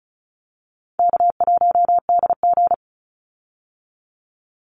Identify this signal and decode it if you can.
Morse code, heard as a keyed tone, about 34 wpm, 700 Hz. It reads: K1DG
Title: Easy Call Signs Repeated 2x - 35wpm